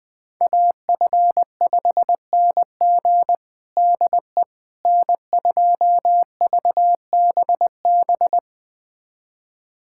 Morse code, heard as a keyed tone, about 21 wpm, 700 Hz. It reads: AF5NG DE N2VBB
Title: USA Callsign DE Callsign - 5 Characters - 20wpm